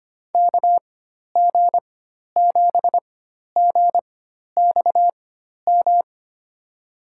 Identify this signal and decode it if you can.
Morse code, heard as a keyed tone, about 25 wpm, spaced as for 12 wpm, 700 Hz. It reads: KG7GXM
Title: USA Call Signs - 4x Character Spacing - 25wpm